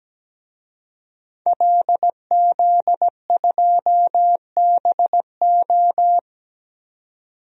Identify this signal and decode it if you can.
Morse code, heard as a keyed tone, about 17 wpm, 700 Hz. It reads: LZ2BO
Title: DXCC-100 Call Signs - 5 Characters - 17wpm